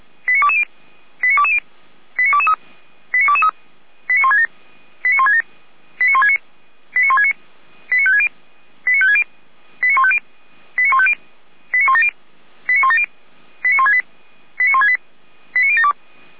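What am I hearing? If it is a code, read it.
Five-tone selective call (ZVEI), heard as quick runs of five tones, tones of about 80 ms. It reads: 89220 89220 89223 89223 89167 89167 89169 89169 89560 89560 89150 89150 89170 89170 89168 89168 89072
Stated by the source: ZVEI five-tone selective call received on SDR.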